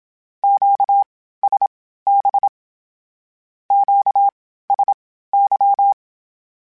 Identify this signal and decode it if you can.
Morse code, heard as a keyed tone, about 27 wpm, spaced as for 15 wpm, 800 Hz. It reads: QSB QSY
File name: (spaced) / Q-Calls Spaced-mp3s / Q Calls Twice sp @25